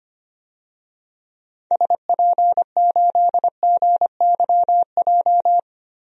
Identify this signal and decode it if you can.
Morse code, heard as a keyed tone, about 25 wpm, 700 Hz. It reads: SP8GYJ
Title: DXCC-100 Call Signs - 6 Characters - Repeated 2x - 25wpm